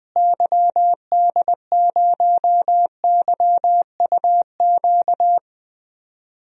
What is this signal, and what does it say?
Morse code, heard as a keyed tone, about 20 wpm, 700 Hz. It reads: YD0YUQ
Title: DXCC-100 Call Signs - 6 Characters - Repeated 2x - 20wpm